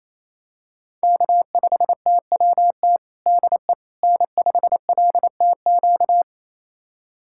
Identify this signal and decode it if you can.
Morse code, heard as a keyed tone, about 28 wpm, 700 Hz. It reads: K5TWT DE N5LTQ
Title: USA Callsign DE Callsign - 5 Characters - 28wpm